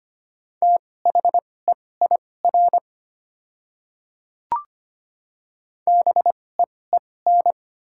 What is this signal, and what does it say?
Morse code, heard as a keyed tone, about 25 wpm, spaced as for 18 wpm, 690 Hz. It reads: THEIR BEEN
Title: 2x Character Spacing - Top 100 Words - 25wpm